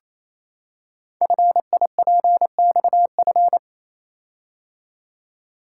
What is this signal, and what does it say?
Morse code, heard as a keyed tone, about 29 wpm, 700 Hz. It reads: FIPXF
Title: Code Groups - 5 Letters - 28wpm